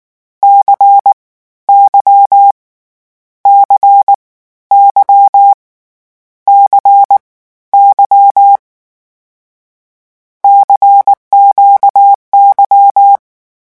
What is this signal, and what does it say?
Morse code, heard as a keyed tone, about 19 wpm, 800 Hz. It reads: C Y C Y C Y CQY
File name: morse mp3 / lecon2-vitesse 18